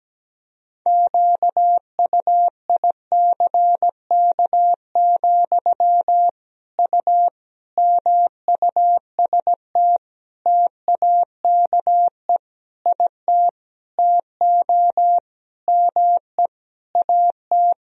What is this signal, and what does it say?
Morse code, heard as a keyed tone, about 17 wpm, 700 Hz. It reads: QUICK, U MUST TAKE IT TO ME AT
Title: Sentences from Top 500 Words and Abbreviations 17wpm